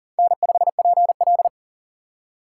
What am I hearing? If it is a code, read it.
Morse code, heard as a keyed tone, about 40 wpm, 700 Hz. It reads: N5PL
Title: USA Call Signs - 4 Characters - 3x Character Spacing - 40wpm